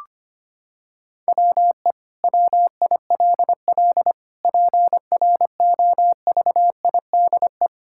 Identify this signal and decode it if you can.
Morse code, heard as a keyed tone, about 25 wpm, 700 Hz. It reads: WE WILL PROVIDE
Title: Sentences from Top 1000 Words - Review Part 5 - 25wpm